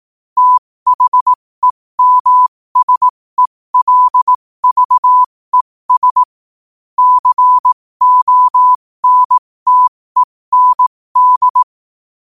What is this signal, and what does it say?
Morse code, heard as a keyed tone, about 18 wpm, spaced as for 15 wpm, 1000 Hz. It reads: THEMSELVES CONTEND
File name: qotdfiles / 15wpm